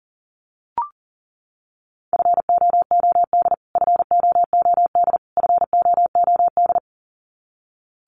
Morse code, heard as a keyed tone, about 40 wpm, 700 Hz. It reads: FOOD FOOD FOOD
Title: Top 500 Words Repeated 40wpm